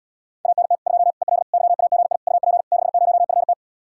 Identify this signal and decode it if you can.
Morse code, heard as a keyed tone, about 80 wpm, 700 Hz. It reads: ANT 2 EL YAGI UP 60FT